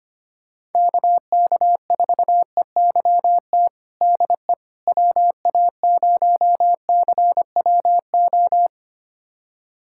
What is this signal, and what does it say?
Morse code, heard as a keyed tone, about 25 wpm, 700 Hz. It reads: KK4EYT DE WA0CWO
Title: USA Callsign DE Callsign - 6 Characters - 25wpm